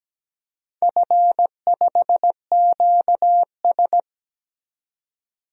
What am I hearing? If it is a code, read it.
Morse code, heard as a keyed tone, about 17 wpm, 700 Hz. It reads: F5QS
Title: DXCC-100 Call Signs - 4 Characters - 17wpm